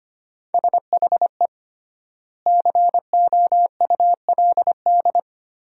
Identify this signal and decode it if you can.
Morse code, heard as a keyed tone, about 25 wpm, 700 Hz. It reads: SHE COULD
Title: 3x Word-Spacing - Sentences from Top 500 Words - 25wpm